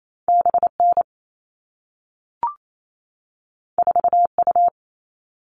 Morse code, heard as a keyed tone, about 28 wpm, 700 Hz. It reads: BN 4U
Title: All Cut Numbers - 2-Digit 28wpm